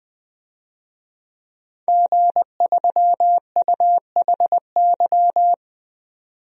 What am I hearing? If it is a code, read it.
Morse code, heard as a keyed tone, about 20 wpm, 700 Hz. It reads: G3UHY
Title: DXCC-100 Call Signs - 5 Characters - 20wpm